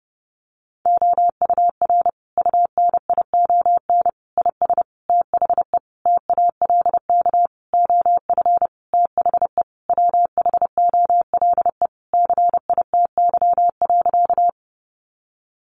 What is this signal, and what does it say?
Morse code, heard as a keyed tone, about 30 wpm, 700 Hz. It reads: OUR UNION IS THE TALK OF THE WHOLE CITY.
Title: Sentences from Top 1400 Words 30wpm